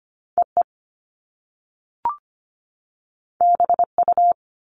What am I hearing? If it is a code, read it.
Morse code, heard as a keyed tone, about 25 wpm, 700 Hz. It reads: EE BU
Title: Top Two Letter Combinations in Words 25wpm